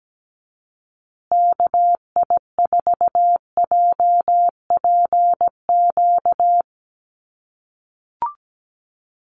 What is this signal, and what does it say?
Morse code, heard as a keyed tone, about 17 wpm, 700 Hz. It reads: KI4JPQ
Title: USA Call Signs Repeated 3x 17wpm